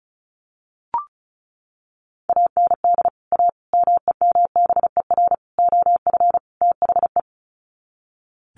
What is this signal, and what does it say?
Morse code, heard as a keyed tone, about 34 wpm, 700 Hz. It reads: AND A MEMBER OF THE
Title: Sets of 5 Words 35wpm